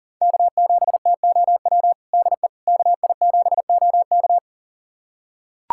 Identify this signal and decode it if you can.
Morse code, heard as a keyed tone, about 40 wpm, 700 Hz. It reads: K7TOW DE KI7OK
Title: USA Callsign DE Callsign - 5 Characters - 40wpm